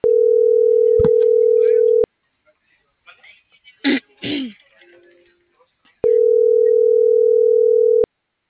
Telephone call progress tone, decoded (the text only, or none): ringback tone